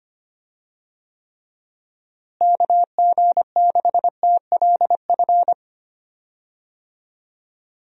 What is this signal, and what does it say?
Morse code, heard as a keyed tone, about 25 wpm, 700 Hz. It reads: KG6TLF
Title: USA Call Signs - 6 Characters - Repeated 3x - 25wpm